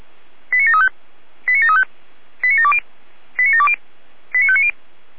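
Five-tone selective call (ZVEI), heard as quick runs of five tones, tones of about 70 ms. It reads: 89735 89735 89720 89720 89590
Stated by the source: ZVEI five-tone selective call received by SDR.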